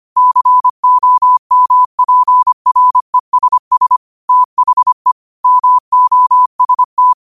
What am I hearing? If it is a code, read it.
Morse code, heard as a keyed tone, about 25 wpm, 1000 Hz. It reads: COMPRESS THE MOST
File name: qotdfiles / 25wpm